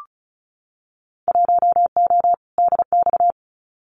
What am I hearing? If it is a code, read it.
Morse code, heard as a keyed tone, about 36 wpm, 700 Hz. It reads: 1O DX
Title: Winter Field Day Exchanges 35wpm